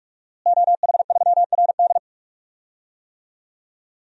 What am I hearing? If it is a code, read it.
Morse code, heard as a keyed tone, about 47 wpm, 700 Hz. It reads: OH3RD
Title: DXCC-100 Call Signs - 5 Characters - Repeated 3x - 45wpm